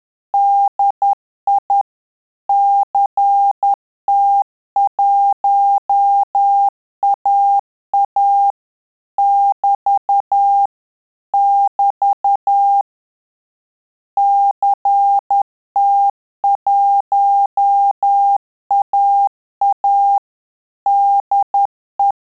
Morse code, heard as a keyed tone, about 10 wpm, 800 Hz. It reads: DI CT1AA <BT> <BT> CT1AA DE
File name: Telegraphy-group / QSO CW 10 PPM